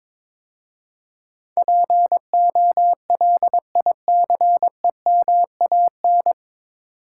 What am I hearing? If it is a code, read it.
Morse code, heard as a keyed tone, about 22 wpm, 700 Hz. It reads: POLICEMAN
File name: 9 Letter Words - First 3 Letters Given - 22wpm